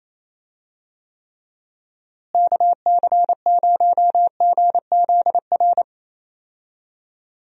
Morse code, heard as a keyed tone, about 27 wpm, 700 Hz. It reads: KC0GZR
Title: USA Call Signs - 6 Characters - Repeated 2x - 28wpm